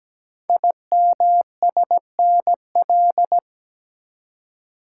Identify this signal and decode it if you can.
Morse code, heard as a keyed tone, about 17 wpm, 700 Hz. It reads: IMSNL
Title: Code Groups - 5 Letters - 17wpm